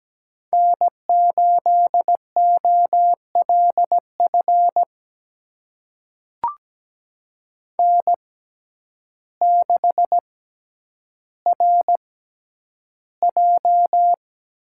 Morse code, heard as a keyed tone, about 17 wpm, spaced as for 9 wpm, 700 Hz. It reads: N8OLF N6RJ
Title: USA Call Signs - 5 Characters - 6x Character Spacing - 17wpm